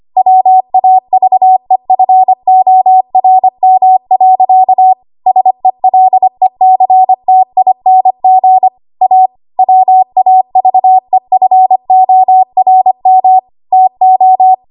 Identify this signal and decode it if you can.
Morse code, heard as a keyed tone, about 25 wpm, 750 Hz. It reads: WAVEFORM. SELECTING A WAVEFORM TO
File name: Archive / 25 WPM / 25WPM